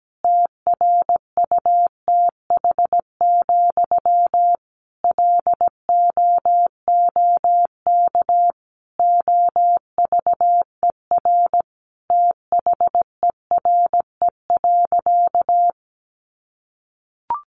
Morse code, heard as a keyed tone, about 17 wpm, 700 Hz. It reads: TRUTH, LOOK OVER THERE.
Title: Sentences from Top 800 Words 17wpm